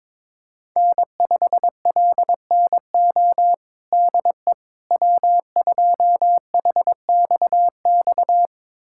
Morse code, heard as a keyed tone, about 22 wpm, 700 Hz. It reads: N5LNO DE W2HXX
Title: USA Callsign DE Callsign - 5 Characters - 22wpm